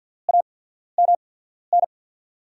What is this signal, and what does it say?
Morse code, heard as a keyed tone, about 53 wpm, spaced as for 14 wpm, 700 Hz. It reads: AMN
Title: USA Call Signs - 5 Characters - 8x Character Spacing - 50wpm